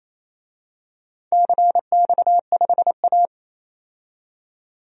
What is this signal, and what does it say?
Morse code, heard as a keyed tone, about 28 wpm, 700 Hz. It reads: CX5A
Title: DXCC-100 Call Signs - 4 Characters - 28wpm